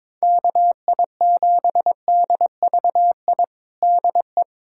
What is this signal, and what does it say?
Morse code, heard as a keyed tone, about 21 wpm, 700 Hz. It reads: KI7DVI DE
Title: USA Callsign DE Callsign - 6 Characters - 22wpm